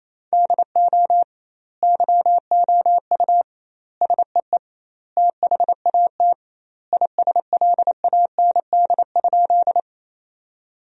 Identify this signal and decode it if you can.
Morse code, heard as a keyed tone, about 28 wpm, 700 Hz. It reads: DO YOU SEE THAT ISLAND?
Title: 2x Word-Spacing - Sentences from Top 500 Words - 28wpm